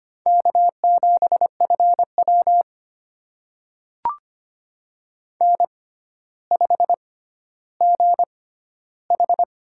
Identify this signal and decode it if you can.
Morse code, heard as a keyed tone, about 25 wpm, spaced as for 13 wpm, 700 Hz. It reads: K7FW N5GH
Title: USA Call Signs - 4 Characters - 6x Character Spacing - 25wpm